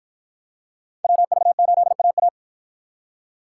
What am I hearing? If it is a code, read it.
Morse code, heard as a keyed tone, about 53 wpm, 700 Hz. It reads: WV8AA